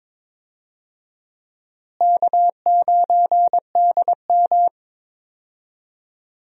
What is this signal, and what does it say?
Morse code, heard as a keyed tone, about 22 wpm, 700 Hz. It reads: K9DM
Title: USA Call Signs - 4 Characters - Repeated 3x - 22wpm